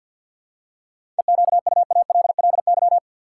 Morse code, heard as a keyed tone, about 49 wpm, 700 Hz. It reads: EQUALLY
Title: Seven Letter Words - 50wpm